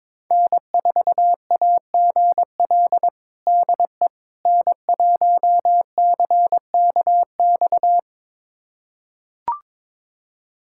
Morse code, heard as a keyed tone, about 22 wpm, 700 Hz. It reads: N4AGL DE N1CKX
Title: USA Callsign DE Callsign - 5 Characters - 22wpm